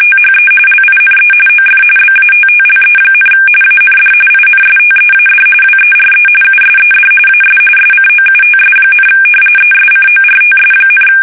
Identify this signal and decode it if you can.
RTTY, about 45.45 baud. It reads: THE LINE OF BATTLE, AND INSTEAD WERE USED AS SCOUTS, ESCORTS,
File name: fsk850 / battleshipnj